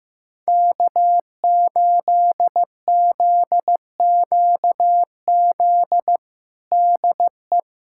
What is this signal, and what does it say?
Morse code, heard as a keyed tone, about 15 wpm, 700 Hz. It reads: K8ZQZ DE
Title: USA Callsign DE Callsign - 5 Characters - 15wpm